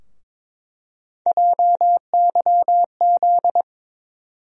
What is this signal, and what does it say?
Morse code, heard as a keyed tone, about 22 wpm, 700 Hz. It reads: JYZ
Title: USA Call Sign Postfixes - 3 Letters 22wpm